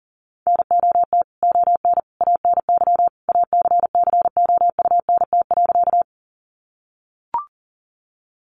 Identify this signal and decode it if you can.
Morse code, heard as a keyed tone, about 40 wpm, 700 Hz. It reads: NOT ON ANY ACCOUNT.
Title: Sentences from Top 900 Words 40wpm